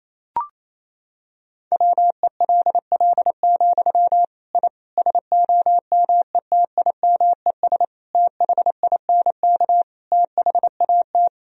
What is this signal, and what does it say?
Morse code, heard as a keyed tone, about 27 wpm, 700 Hz. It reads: WELL, I SOMETIMES THINK THAT
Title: Sentences from Top 1000 Words - Review Part 3 - 28wpm